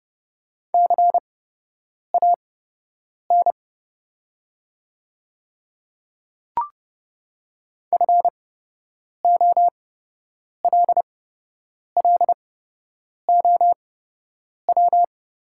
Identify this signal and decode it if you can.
Morse code, heard as a keyed tone, about 30 wpm, spaced as for 8 wpm, 700 Hz. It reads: CAN FOLLOW
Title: 8x Character Spacing - Top 300 Words - 30wpm